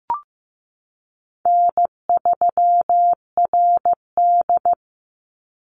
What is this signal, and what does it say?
Morse code, heard as a keyed tone, about 15 wpm, 700 Hz. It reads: N3RD
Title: Easy Call Signs Repeated 2x - 15wpm